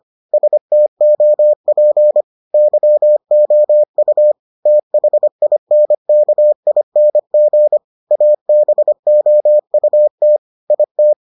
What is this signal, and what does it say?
Morse code, heard as a keyed tone, about 25 wpm, 600 Hz. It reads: STOP YOU THINKING ABOUT IT